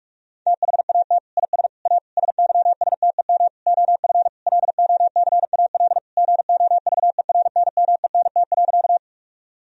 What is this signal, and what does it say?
Morse code, heard as a keyed tone, about 45 wpm, 700 Hz. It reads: THAT IS A SYSTEM OF LOCAL GOVERNMENT.
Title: Sentences from Top 1000 Words - Review Part 3 - 45wpm